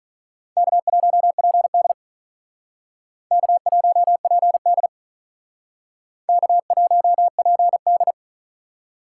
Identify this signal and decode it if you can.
Morse code, heard as a keyed tone, about 43 wpm, 700 Hz. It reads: K1PD K1PD K1PD
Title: Speed Racing - USA Call Signs 35wpm